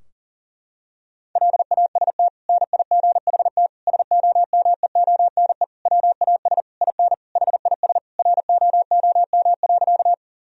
Morse code, heard as a keyed tone, about 40 wpm, 700 Hz. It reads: LAST NIGHT SOMEONE WAS IN HIS ROOM.
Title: Sentences from Top 1000 Words - Review Part 4 - 40wpm